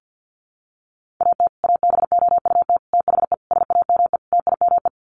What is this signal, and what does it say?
Morse code, heard as a keyed tone, about 50 wpm, 700 Hz. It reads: AT ABOUT THE SAME TIME
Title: Sets of 5 Words Encore 50wpm